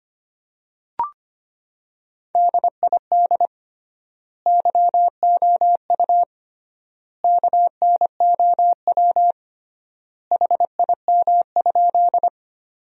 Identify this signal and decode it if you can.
Morse code, heard as a keyed tone, about 25 wpm, 700 Hz. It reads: DID YOU KNOW HIM?
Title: 3x Word-Spacing - Sentences from Top 100 Words - 25wpm